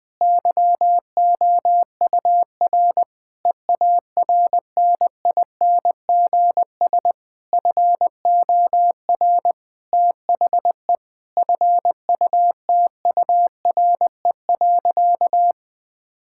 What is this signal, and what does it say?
Morse code, heard as a keyed tone, about 20 wpm, 700 Hz. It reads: YOUR EARNINGS FOR THE FUTURE.
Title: Difficult Sentences with Hints - 20wpm